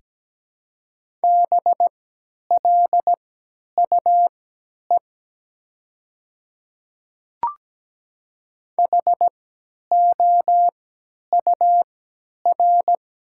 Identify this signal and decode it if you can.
Morse code, heard as a keyed tone, about 17 wpm, spaced as for 10 wpm, 700 Hz. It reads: BLUE HOUR
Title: 3x Character Spacing - Top 500 Words - 17wpm